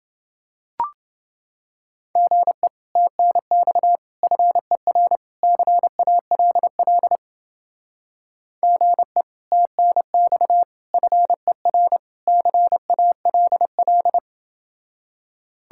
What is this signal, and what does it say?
Morse code, heard as a keyed tone, about 29 wpm, 700 Hz. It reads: GE TNX FER CALL GE TNX FER CALL